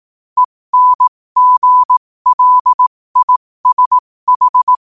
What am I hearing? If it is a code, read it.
Morse code, heard as a keyed tone, about 18 wpm, spaced as for 15 wpm, 1000 Hz. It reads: ENGLISH
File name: qotdfiles / 15wpm